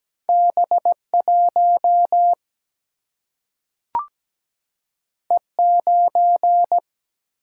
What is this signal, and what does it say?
Morse code, heard as a keyed tone, about 17 wpm, 700 Hz. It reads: B1 E9
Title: Single-Letter Single-Number 17wpm